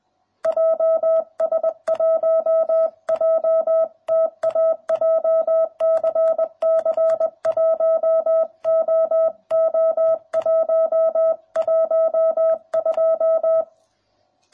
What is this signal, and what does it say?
Morse code, heard as a keyed tone, about 20 wpm, 660 Hz. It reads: JS1JTA JCC 1OO112